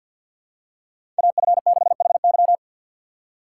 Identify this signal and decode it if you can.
Morse code, heard as a keyed tone, about 48 wpm, 700 Hz. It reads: AF6HY
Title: USA Call Signs - 3x Character Spacing - 50wpm